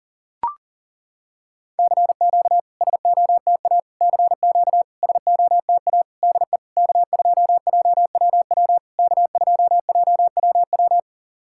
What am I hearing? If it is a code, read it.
Morse code, heard as a keyed tone, about 40 wpm, 690 Hz. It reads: CQ SOTA CQ SOTA DE K2JWW K2JWW
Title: SOTA QSO Practice 40wpm